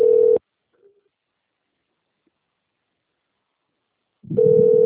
Ringback tone (North America), 440 + 480 Hz.